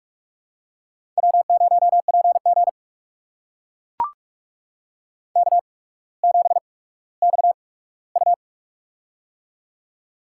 Morse code, heard as a keyed tone, about 44 wpm, spaced as for 19 wpm, 700 Hz. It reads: W0PG K7XU
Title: USA Call Signs - 4 Characters - 8x Character Spacing - 45wpm